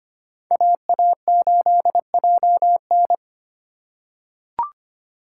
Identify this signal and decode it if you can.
Morse code, heard as a keyed tone, about 25 wpm, 700 Hz. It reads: AA8JN
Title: USA Call Signs - 5 Characters - 3x Character Spacing - 25wpm